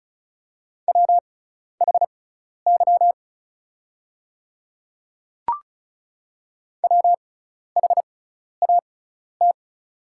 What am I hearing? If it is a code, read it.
Morse code, heard as a keyed tone, about 34 wpm, spaced as for 12 wpm, 700 Hz. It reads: WHY WHAT
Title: 6x Character Spacing - Top 200 Words - 35wpm